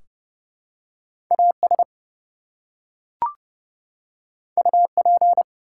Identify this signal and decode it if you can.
Morse code, heard as a keyed tone, about 30 wpm, 700 Hz. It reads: AS UP
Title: Two Letter Words - 30wpm